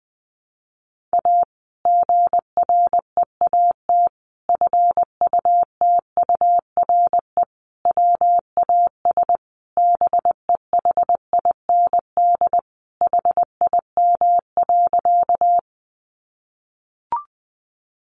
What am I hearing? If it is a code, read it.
Morse code, heard as a keyed tone, about 20 wpm, 700 Hz. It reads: A GREAT FUTURE WAS BEHIND HIM.
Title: Sentences from Top 700 Words 20wpm